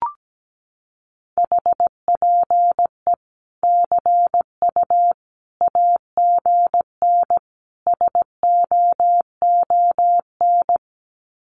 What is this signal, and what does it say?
Morse code, heard as a keyed tone, about 17 wpm, 700 Hz. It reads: HPE CU AGN SOON
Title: Common QSO Phrases 17wpm